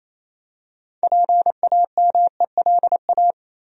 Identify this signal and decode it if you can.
Morse code, heard as a keyed tone, about 28 wpm, 700 Hz. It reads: PAMELA
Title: USA Common Names - 28wpm